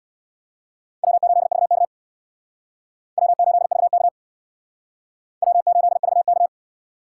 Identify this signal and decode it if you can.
Morse code, heard as a keyed tone, about 62 wpm, 700 Hz. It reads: W8VK W8VK W8VK
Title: Speed Racing - USA 1x2 Call Signs 50wpm